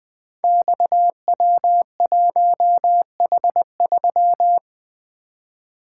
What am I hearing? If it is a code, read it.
Morse code, heard as a keyed tone, about 20 wpm, 700 Hz. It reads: XW1H3
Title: Code Groups - 5 Letter:Numbers - 20wpm